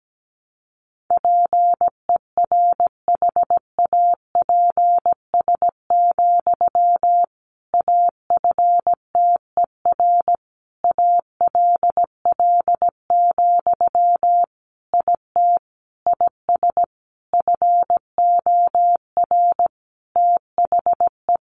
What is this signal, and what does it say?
Morse code, heard as a keyed tone, about 17 wpm, 700 Hz. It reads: PERHAPS, AFTER ALL, IT IS FOR THE
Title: Sentences from Top 500 Words 17wpm